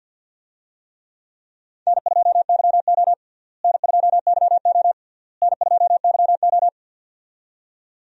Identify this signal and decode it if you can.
Morse code, heard as a keyed tone, about 48 wpm, 700 Hz. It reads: N2YO N2YO N2YO
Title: USA Call Signs - 4 Characters - Repeated 3x - 50wpm